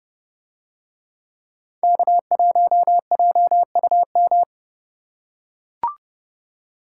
Morse code, heard as a keyed tone, about 31 wpm, 700 Hz. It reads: K1JUM